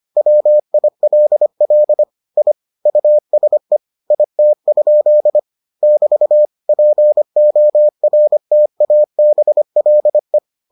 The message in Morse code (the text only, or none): WILL I USE IT? <BT> PORTABLE